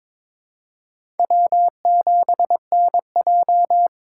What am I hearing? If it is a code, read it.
Morse code, heard as a keyed tone, about 22 wpm, 700 Hz. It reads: W7NJ
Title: USA Call Signs - 3x Character Spacing - 22wpm